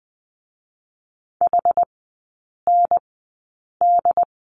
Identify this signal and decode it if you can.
Morse code, heard as a keyed tone, about 20 wpm, spaced as for 8 wpm, 700 Hz. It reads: HND
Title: 067 Lesson 12e - Letter d - Three Characters 20wpm